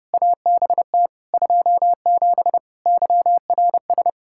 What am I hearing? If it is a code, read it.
Morse code, heard as a keyed tone, about 30 wpm, 700 Hz. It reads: ABT 27 YRS
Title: K1USN SST QSO Practice 30wpm